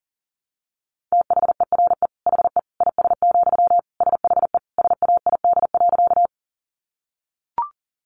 Morse code, heard as a keyed tone, about 40 wpm, 700 Hz. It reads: THERE HE IS, SHE SAID.
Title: Sentences from Top 100 Words 40wpm